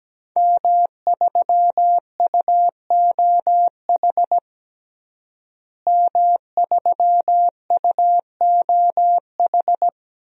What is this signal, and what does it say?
Morse code, heard as a keyed tone, about 17 wpm, 700 Hz. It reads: M3UOH M3UOH
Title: DXCC-100 Call Signs - 5 Characters - Repeated 2x - 17wpm